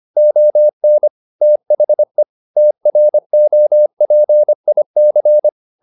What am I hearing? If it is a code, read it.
Morse code, heard as a keyed tone, about 25 wpm, 600 Hz. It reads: ON THE TROPIC
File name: the old man and the sea part 1 25wpm